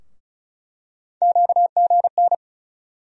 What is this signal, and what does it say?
Morse code, heard as a keyed tone, about 36 wpm, 700 Hz. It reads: QGN
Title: USA Call Sign Postfixes - 3 Letters 35wpm